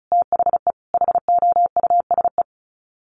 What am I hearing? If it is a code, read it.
Morse code, heard as a keyed tone, about 34 wpm, 700 Hz. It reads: THE HOUSE
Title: Sets of 3 Words 35wpm